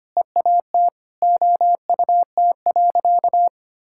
Morse code, heard as a keyed tone, about 25 wpm, 700 Hz. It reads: EAT OUT.
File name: practice / 25wpm / Sentences with 4-Word Phrases - 25wpm